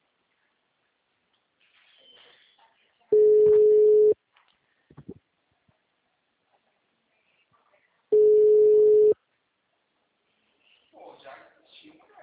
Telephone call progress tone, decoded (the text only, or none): ringback tone